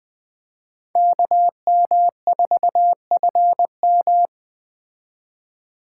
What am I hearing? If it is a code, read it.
Morse code, heard as a keyed tone, about 21 wpm, 700 Hz. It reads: KM4FM